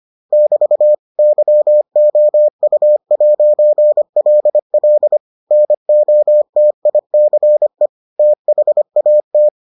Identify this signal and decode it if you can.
Morse code, heard as a keyed tone, about 25 wpm, 600 Hz. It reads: <BT> YOU'LL NOTICE THAT